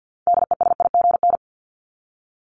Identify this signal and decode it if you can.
Morse code, heard as a keyed tone, about 49 wpm, 700 Hz. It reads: DESIGN
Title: ED Suffix - Top 100 Words with Base Words 50wpm